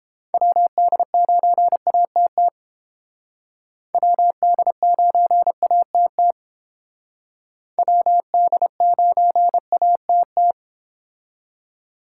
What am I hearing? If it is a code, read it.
Morse code, heard as a keyed tone, about 29 wpm, 700 Hz. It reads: WD9ATT WD9ATT WD9ATT